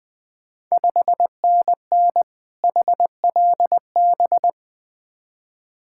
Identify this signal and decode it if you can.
Morse code, heard as a keyed tone, about 20 wpm, 700 Hz. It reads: 5NN HLB
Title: Ontario QSO Party Exchange 20wpm